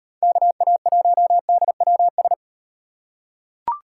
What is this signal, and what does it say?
Morse code, heard as a keyed tone, about 37 wpm, 700 Hz. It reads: KA1DWS